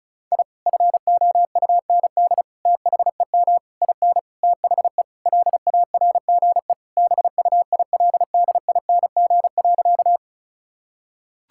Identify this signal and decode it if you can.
Morse code, heard as a keyed tone, about 36 wpm, 700 Hz. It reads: I FOUND THEM IN THE LARGE BUILDING.
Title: Sentences from Top 1000 Words - Review Part 5 - 35wpm